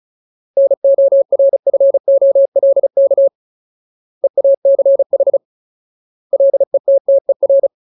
Morse code, heard as a keyed tone, about 34 wpm, 550 Hz. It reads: NORFOLK EACH LETTER
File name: Ultimate Word List / 35WPM Ultimate